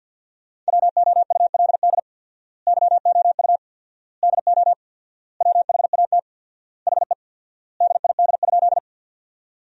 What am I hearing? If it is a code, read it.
Morse code, heard as a keyed tone, about 50 wpm, 700 Hz. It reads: WOULD YOU DO WHAT HE DID?
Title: 4x Word-Spacing - Sentences from Top 100 Words - 50wpm